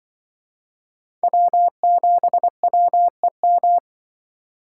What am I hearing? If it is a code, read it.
Morse code, heard as a keyed tone, about 25 wpm, 700 Hz. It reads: W7WEM
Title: Speed Racing - USA 1x3 Call Signs 20wpm